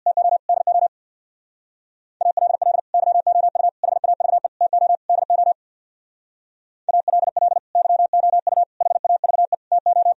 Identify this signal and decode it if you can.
Morse code, heard as a keyed tone, about 64 wpm, 700 Hz. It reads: TO DO ALL YOU HAVE TO DO ALL YOU HAVE TO